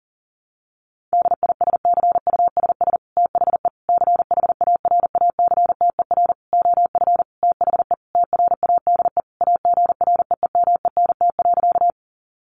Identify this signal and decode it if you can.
Morse code, heard as a keyed tone, about 40 wpm, 700 Hz. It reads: DISCUSS THE CHARACTER OF THE TRADE AGREEMENT.
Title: Sentences from Top 1500 Words 40wpm